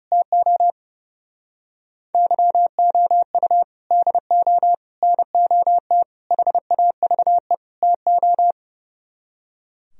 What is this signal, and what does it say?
Morse code, heard as a keyed tone, about 40 wpm, 700 Hz. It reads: TO YOU DO NOT HAVE TO